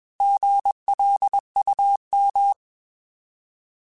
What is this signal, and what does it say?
Morse code, heard as a keyed tone, about 20 wpm, 800 Hz. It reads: GLUM
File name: Alphabetical / Alphabetical mp3s / Gs @20